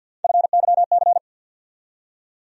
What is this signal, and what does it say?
Morse code, heard as a keyed tone, about 53 wpm, 700 Hz. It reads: FYC
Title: USA Call Sign Postfixes - 3 Letters 50wpm